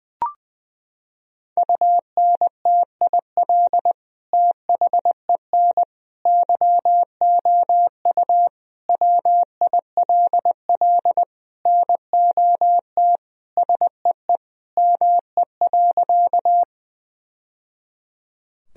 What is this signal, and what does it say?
Morse code, heard as a keyed tone, about 21 wpm, 700 Hz. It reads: UNTIL THEN YOU WILL NOT SEE ME.
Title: Sentences from Top 1000 Words - Review Part 1 - 20wpm